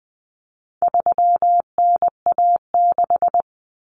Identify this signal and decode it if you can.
Morse code, heard as a keyed tone, about 20 wpm, 700 Hz. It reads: 3NA6
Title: Common Cut Numbers - 4-Digit 20wpm